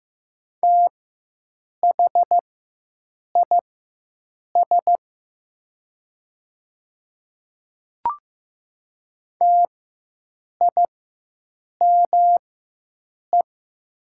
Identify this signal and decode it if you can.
Morse code, heard as a keyed tone, about 15 wpm, spaced as for 7 wpm, 700 Hz. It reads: THIS TIME
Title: 4x Character Spacing - Top 100 Words - 15wpm